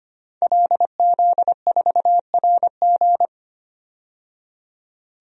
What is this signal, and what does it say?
Morse code, heard as a keyed tone, about 25 wpm, 700 Hz. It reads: LZ4RG
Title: DXCC-100 Call Signs - 5 Characters - 25wpm